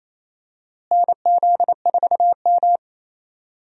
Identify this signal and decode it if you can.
Morse code, heard as a keyed tone, about 28 wpm, 700 Hz. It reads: NZ4M
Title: USA Call Signs - 4 Characters - Repeated 2x - 28wpm